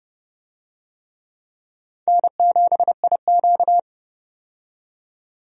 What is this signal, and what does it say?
Morse code, heard as a keyed tone, about 30 wpm, 700 Hz. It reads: N7IQ
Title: USA Call Signs - 4 Characters - 30wpm